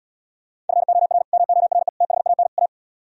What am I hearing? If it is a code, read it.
Morse code, heard as a keyed tone, about 77 wpm, 700 Hz. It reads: FOR MORE THAN A